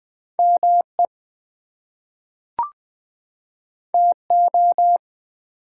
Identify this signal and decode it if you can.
Morse code, heard as a keyed tone, about 20 wpm, 700 Hz. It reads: ME TO
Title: Two Letter Words - 20wpm